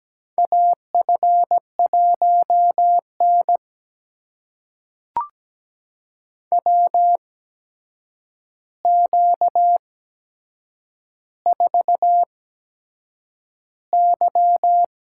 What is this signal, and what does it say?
Morse code, heard as a keyed tone, about 17 wpm, spaced as for 7 wpm, 700 Hz. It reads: AF1N WQ4Y
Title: USA Call Signs - 4 Characters - 8x Character Spacing - 17wpm